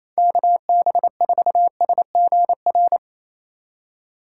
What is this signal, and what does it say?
Morse code, heard as a keyed tone, about 28 wpm, 700 Hz. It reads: KB4SGR
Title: USA Call Signs - 6 Characters - 3x Character Spacing - 28wpm